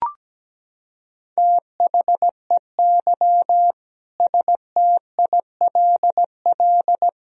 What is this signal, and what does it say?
Morse code, heard as a keyed tone, about 17 wpm, 700 Hz. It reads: THEY STILL
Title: Sentences from Top 1000 Words - Review Part 4 - 17wpm